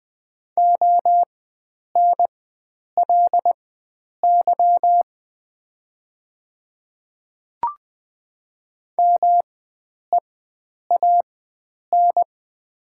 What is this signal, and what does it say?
Morse code, heard as a keyed tone, about 20 wpm, spaced as for 9 wpm, 700 Hz. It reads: ONLY MEAN
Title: 4x Character Spacing - Top 500 Words - 20wpm